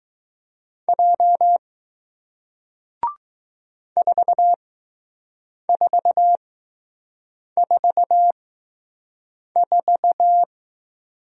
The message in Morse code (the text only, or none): J 4444